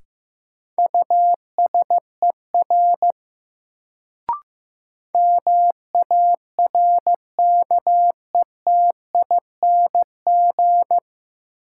Morse code, heard as a keyed tone, about 15 wpm, 700 Hz. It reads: USER MARKETING
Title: Sentences from Top 1000 Words - Review Part 5 - New Word Drill - 15wpm